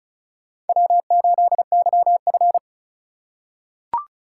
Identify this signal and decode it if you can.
Morse code, heard as a keyed tone, about 36 wpm, 690 Hz. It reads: W8YF
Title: USA Call Signs - 4 Characters - 2x Character Spacing - 35wpm